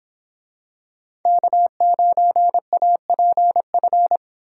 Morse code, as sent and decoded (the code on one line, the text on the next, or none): -.- ----. .- .--. ..-.
K9APF